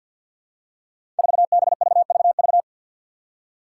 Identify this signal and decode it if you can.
Morse code, heard as a keyed tone, about 52 wpm, 700 Hz. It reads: 4BVVV
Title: Five Challenging Letter-Number Combinations 50wpm